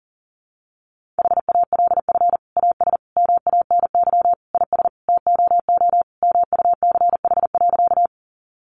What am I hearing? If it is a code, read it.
Morse code, heard as a keyed tone, about 40 wpm, 700 Hz. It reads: HALF AS MANY IS TOO MUCH.
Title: Sentences from Top 400 Words 40wpm